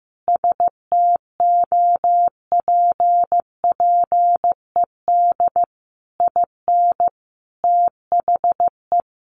Morse code, heard as a keyed tone, about 15 wpm, 700 Hz. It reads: STOPPED IN THE
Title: A Sound Of Thunder 15wpm